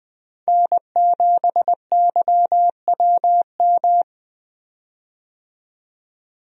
Morse code, heard as a keyed tone, about 20 wpm, 700 Hz. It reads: N7YWM
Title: USA Call Signs - 5 Characters - 20wpm